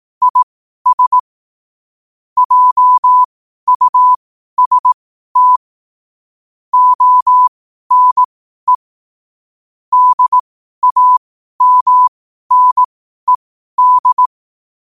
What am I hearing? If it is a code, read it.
Morse code, heard as a keyed tone, about 18 wpm, spaced as for 12 wpm, 1000 Hz. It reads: IS JUST ONE DAMNED
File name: qotdfiles / 12wpm